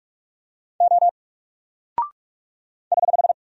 Morse code, heard as a keyed tone, about 44 wpm, 700 Hz. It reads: O <HH>
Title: Single Character (Letter:Number:Punctuation:Prosign) 45wpm